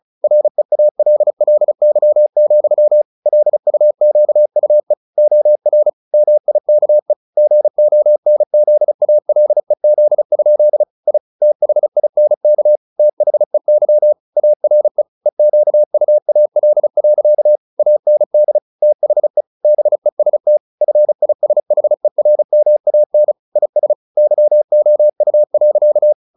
Morse code, heard as a keyed tone, about 34 wpm, 600 Hz. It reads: REALLY, LUQUE OR MIKE GONZALEZ? I THINK THEY ARE EQUAL. AND THE BEST FISHERMAN IS YOU.
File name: the old man and the sea / the old man and the sea 1 35wpm